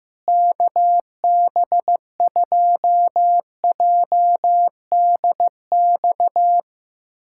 Morse code, heard as a keyed tone, about 15 wpm, 700 Hz. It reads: KB2JDX
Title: USA Callsign DE Callsign 15wpm